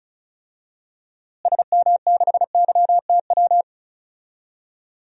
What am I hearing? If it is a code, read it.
Morse code, heard as a keyed tone, about 34 wpm, 700 Hz. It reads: SM6YTW
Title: DXCC-100 Call Signs - 6 Characters - Repeated 2x - 35wpm